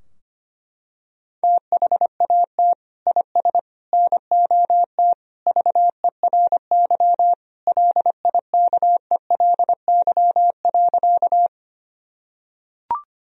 Morse code, heard as a keyed tone, about 25 wpm, 700 Hz. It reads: THAT IS NOT VERY LIKELY.
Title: Sentences from Top 1000 Words - Review Part 4 - 25wpm